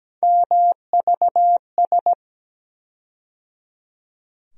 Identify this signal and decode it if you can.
Morse code, heard as a keyed tone, about 17 wpm, 700 Hz. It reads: MVS
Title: USA Call Sign Postfixes - 3 Letters 17wpm